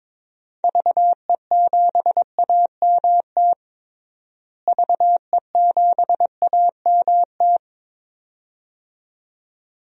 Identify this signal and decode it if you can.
Morse code, heard as a keyed tone, about 21 wpm, 700 Hz. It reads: VE7AMT VE7AMT
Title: DXCC-100 Call Signs - 6 Characters - Repeated 2x - 22wpm